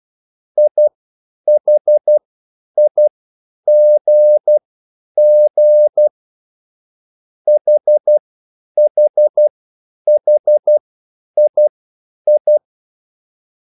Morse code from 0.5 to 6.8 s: IHIGG